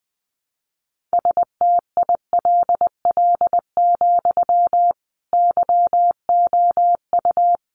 Morse code, heard as a keyed tone, about 20 wpm, 700 Hz. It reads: STILL, YOU
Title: Sentences from Top 300 Words 20wpm